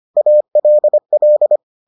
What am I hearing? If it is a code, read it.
Morse code, heard as a keyed tone, about 25 wpm, 600 Hz. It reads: ALL